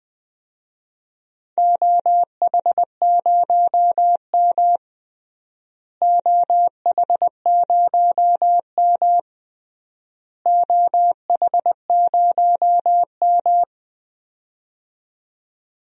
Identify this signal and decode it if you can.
Morse code, heard as a keyed tone, about 21 wpm, 700 Hz. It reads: OH0M OH0M OH0M
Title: DXCC-100 Call Signs - 4 Characters - Repeated 3x - 20wpm